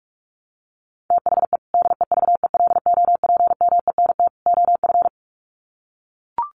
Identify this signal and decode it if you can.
Morse code, heard as a keyed tone, about 44 wpm, 700 Hz. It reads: THE DEVELOPMENT OF
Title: Sets of 3 Words 45wpm